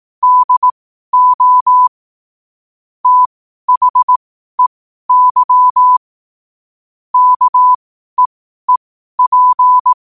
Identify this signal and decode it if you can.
Morse code, heard as a keyed tone, about 18 wpm, spaced as for 12 wpm, 1000 Hz. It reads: DO THEY KEEP
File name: qotdfiles / 12wpm